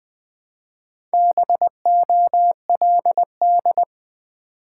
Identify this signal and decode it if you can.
Morse code, heard as a keyed tone, about 20 wpm, 700 Hz. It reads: BOLD
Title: Build A Word - 1 Letter At A Time - 3 to 5-Letter Words - 20wpm